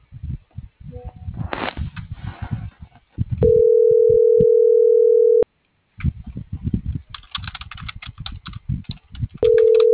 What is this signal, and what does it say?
Ringback tone (North America), 440 + 480 Hz.